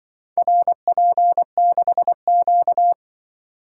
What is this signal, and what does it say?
Morse code, heard as a keyed tone, about 24 wpm, 700 Hz. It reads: RP6Q
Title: Speed-Racing - Four Letter-Number Combinations - 20wpm